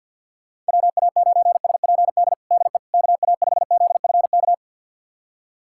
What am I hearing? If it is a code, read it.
Morse code, heard as a keyed tone, about 51 wpm, 690 Hz. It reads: WA9SPD DE KA5ZFK
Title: USA Callsign DE Callsign - 6 Characters - 50wpm